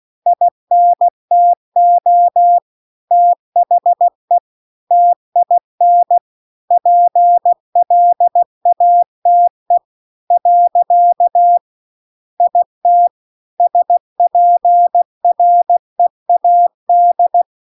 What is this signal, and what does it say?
Morse code, heard as a keyed tone, about 16 wpm, 700 Hz. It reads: INTO THE TIN PLATE . IT SPREAD